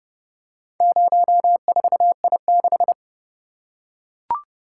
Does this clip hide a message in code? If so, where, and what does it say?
Yes: Morse code from 0.8 to 2.9 s, heard as a keyed tone, about 30 wpm, 700 Hz. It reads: 04I6